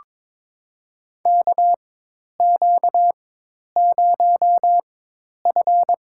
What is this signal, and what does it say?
Morse code, heard as a keyed tone, about 22 wpm, spaced as for 10 wpm, 700 Hz. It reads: KQ0F
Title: USA Call Signs - 4 Characters - 4x Character Spacing - 22wpm